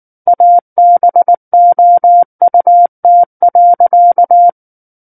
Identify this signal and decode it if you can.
Morse code, heard as a keyed tone, about 19 wpm, 700 Hz. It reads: ABOUT.
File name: Book One / 10-Chapter 10 - 19 WPM